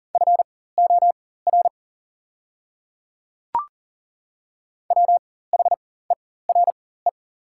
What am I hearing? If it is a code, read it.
Morse code, heard as a keyed tone, about 39 wpm, spaced as for 18 wpm, 700 Hz. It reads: FOR WHERE
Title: 4x Character Spacing - Top 200 Words - 40wpm